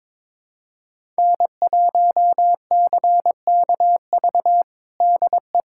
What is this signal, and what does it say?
Morse code, heard as a keyed tone, about 22 wpm, 700 Hz. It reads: N1CKV DE
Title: USA Callsign DE Callsign - 5 Characters - 22wpm